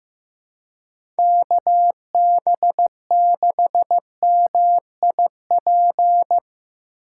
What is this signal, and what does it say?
Morse code, heard as a keyed tone, about 15 wpm, 700 Hz. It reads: KB6MIP
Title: USA Call Signs - 2x Character Spacing - 15wpm